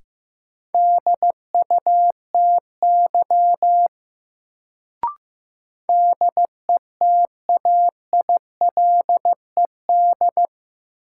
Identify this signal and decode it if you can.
Morse code, heard as a keyed tone, about 15 wpm, 700 Hz. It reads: DUTY DETAILED
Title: Sentences from Top 1500 Words - New Word Drill - 15wpm